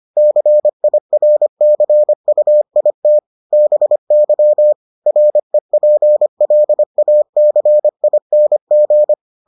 Morse code, heard as a keyed tone, about 25 wpm, 600 Hz. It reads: CIRCUIT BY REPLACING